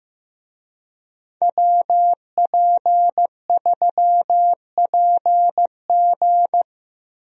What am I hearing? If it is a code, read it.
Morse code, heard as a keyed tone, about 15 wpm, 700 Hz. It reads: WP3PG
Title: USA Call Signs - 5 Characters - Repeated 3x - 15wpm